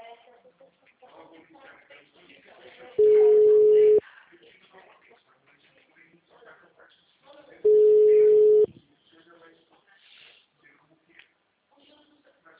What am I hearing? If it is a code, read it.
Ringback tone (Europe, 425 Hz), with speech.